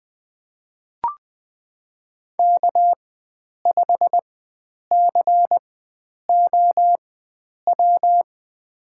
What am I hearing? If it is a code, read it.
Morse code, heard as a keyed tone, about 20 wpm, spaced as for 9 wpm, 700 Hz. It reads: K5COW
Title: USA Call Signs - 5 Characters - 4x Character Spacing - 20wpm